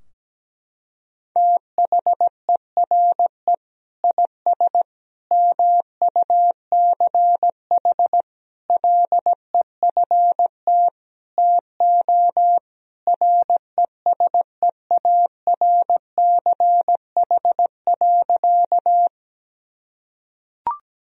Morse code, heard as a keyed tone, about 17 wpm, 700 Hz. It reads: THERE IS MUCH LEFT TO RESEARCH.
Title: Sentences from Top 1000 Words - Review Part 4 - 17wpm